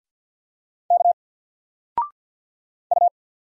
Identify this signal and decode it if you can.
Morse code, heard as a keyed tone, about 49 wpm, 700 Hz. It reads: K U
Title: Single Letter-Number 50wpm